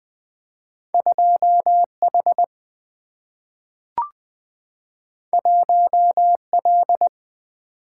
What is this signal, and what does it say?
Morse code, heard as a keyed tone, about 20 wpm, 700 Hz. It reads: 2H 1L
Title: Two Letter-Number Combinations - 20wpm